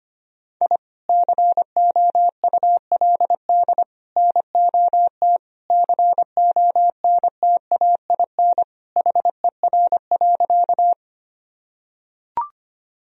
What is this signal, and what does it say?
Morse code, heard as a keyed tone, about 25 wpm, 700 Hz. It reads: I COULD NOT CONTAIN HER.
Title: Sentences from Top 1000 Words - Review Part 2 - 25wpm